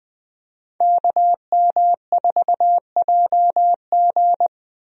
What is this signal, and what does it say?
Morse code, heard as a keyed tone, about 20 wpm, 700 Hz. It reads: KM4JG
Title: USA Call Signs - 5 Characters - Repeated 2x - 20wpm